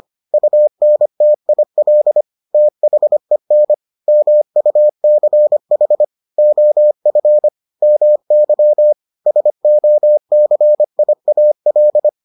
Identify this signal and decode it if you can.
Morse code, heard as a keyed tone, about 25 wpm, 600 Hz. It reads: UNTIL THEN MUCH OF MY SOCIAL